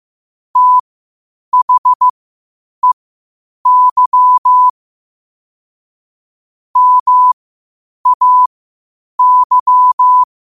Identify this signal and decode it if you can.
Morse code, heard as a keyed tone, about 15 wpm, spaced as for 8 wpm, 1000 Hz. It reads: THEY MAY